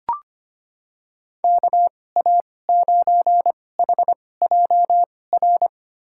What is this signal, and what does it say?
Morse code, heard as a keyed tone, about 25 wpm, spaced as for 18 wpm, 700 Hz. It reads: KA9HJR
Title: USA Call Signs - 6 Characters - 2x Character Spacing - 25wpm